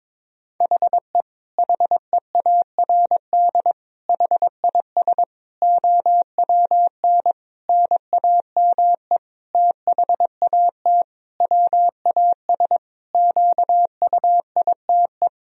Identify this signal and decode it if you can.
Morse code, heard as a keyed tone, about 22 wpm, 700 Hz. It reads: HE HEARD HIS OWN NAME THAT WAS QUITE
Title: Sentences from Top 1000 Words - Review Part 2 - 22wpm